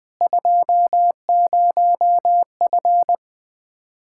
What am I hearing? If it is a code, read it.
Morse code, heard as a keyed tone, about 20 wpm, 700 Hz. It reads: 20F
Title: Speed-Racing Common QSO Phrases Encore - 20wpm